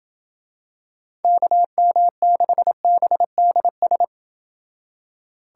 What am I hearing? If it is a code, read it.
Morse code, heard as a keyed tone, about 26 wpm, 700 Hz. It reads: KM6BDS